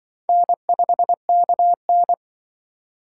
Morse code, heard as a keyed tone, about 25 wpm, 700 Hz. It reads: N5KN
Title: Speed Racing - USA 1x2 Call Signs 20wpm